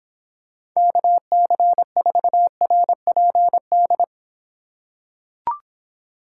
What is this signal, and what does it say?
Morse code, heard as a keyed tone, about 26 wpm, 700 Hz. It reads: KC4RPD